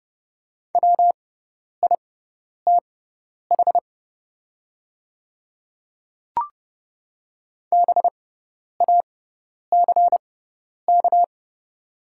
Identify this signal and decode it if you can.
Morse code, heard as a keyed tone, about 30 wpm, spaced as for 10 wpm, 700 Hz. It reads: WITH BACK
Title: 6x Character Spacing - Top 500 Words - 30wpm